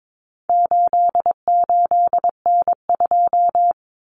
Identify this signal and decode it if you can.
Morse code, heard as a keyed tone, about 22 wpm, 700 Hz. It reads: 88N2
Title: Common Cut Numbers - 4-Digit 22wpm